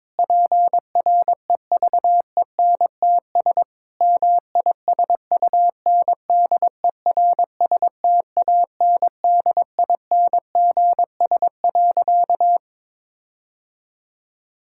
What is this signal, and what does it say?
Morse code, heard as a keyed tone, about 23 wpm, 700 Hz. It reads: PREVENTS MISUNDERSTANDINGS.
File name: Difficult Sentences with Hints - 22wpm